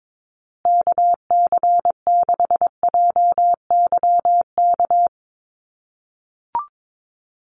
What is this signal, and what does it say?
Morse code, heard as a keyed tone, about 22 wpm, 700 Hz. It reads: KC6JYK
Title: USA Call Signs Repeated 3x 22wpm